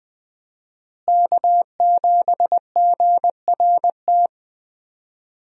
Morse code, heard as a keyed tone, about 20 wpm, 700 Hz. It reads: K7GRT
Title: USA Call Signs - 5 Characters - Repeated 3x - 20wpm